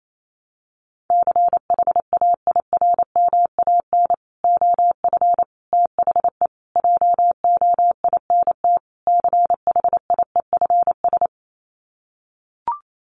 Morse code, heard as a keyed tone, about 28 wpm, 700 Hz. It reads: CHAIRMAN OF THE JOINT CHIEFS
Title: Sets of 5 Words Encore 28wpm